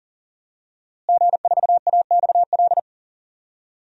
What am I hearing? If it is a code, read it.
Morse code, heard as a keyed tone, about 40 wpm, 700 Hz. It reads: G4AXL
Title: DXCC-100 Call Signs - 5 Characters - Repeated 3x - 40wpm